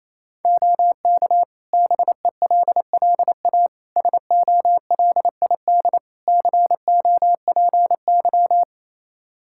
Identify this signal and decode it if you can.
Morse code, heard as a keyed tone, about 27 wpm, 700 Hz. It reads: OK BELLA SOLID COPY
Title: K1USN SST QSO Practice 28wpm